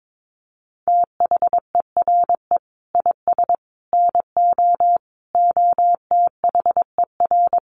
Morse code, heard as a keyed tone, about 22 wpm, 700 Hz. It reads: THERE IS NO OTHER
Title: Sentences from Top 1300 Words 22wpm